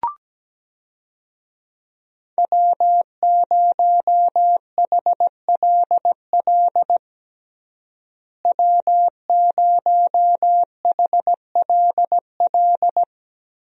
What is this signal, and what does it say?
Morse code, heard as a keyed tone, about 17 wpm, 700 Hz. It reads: W0HLL W0HLL
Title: USA Call Signs - 5 Characters - Repeated 2x - 17wpm